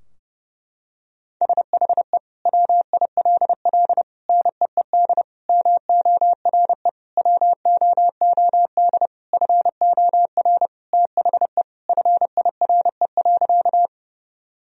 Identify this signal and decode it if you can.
Morse code, heard as a keyed tone, about 30 wpm, 700 Hz. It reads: SHE WILL NEED MORE WOOD FOR THE FIRE.
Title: Sentences from Top 1000 Words - Review Part 2 - 30wpm